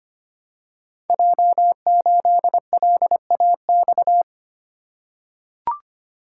Morse code, heard as a keyed tone, about 25 wpm, 700 Hz. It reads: J8LAX
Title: Code Groups - 5 Letter:Numbers - 25wpm